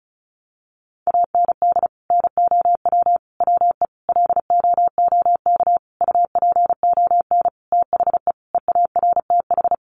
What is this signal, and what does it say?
Morse code, heard as a keyed tone, about 34 wpm, 700 Hz. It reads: AND NOW WE LOOK UPON THE EARTH
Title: Sentences from Top 1200 Words 35wpm